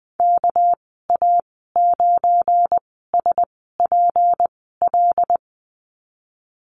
Morse code, heard as a keyed tone, about 20 wpm, spaced as for 15 wpm, 700 Hz. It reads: KA9SPL
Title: 200 Lesson 32g - Letter k - Call Signs 20wpm